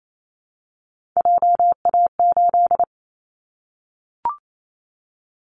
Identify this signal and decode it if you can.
Morse code, heard as a keyed tone, about 28 wpm, 700 Hz. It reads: JA8
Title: Callsign Prefixes for Top 25 DXCC - Repeated 3x - 28wpm